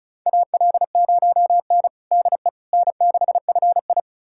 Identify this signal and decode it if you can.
Morse code, heard as a keyed tone, about 36 wpm, 700 Hz. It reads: AL0N DE N6FI
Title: USA Callsign DE Callsign - 4 Characters - 35wpm